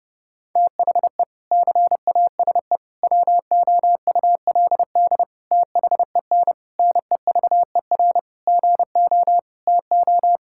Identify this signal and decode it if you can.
Morse code, heard as a keyed tone, about 32 wpm, 700 Hz. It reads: THE CASE WOULD THEN NEVER GO TO
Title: Sentences from Top 1600 Words 30wpm